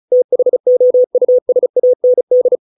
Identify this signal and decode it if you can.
Morse code, heard as a keyed tone, about 34 wpm, 500 Hz. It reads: THOUSAND
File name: 632 words / 632 Words 35WPM